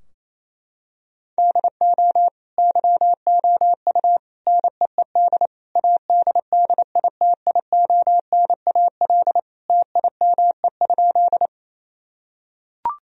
Morse code, heard as a keyed tone, about 28 wpm, 700 Hz. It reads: DO YOU NEED ADDITIONAL TIME?
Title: Sentences from Top 1000 Words - Review Part 4 - 28wpm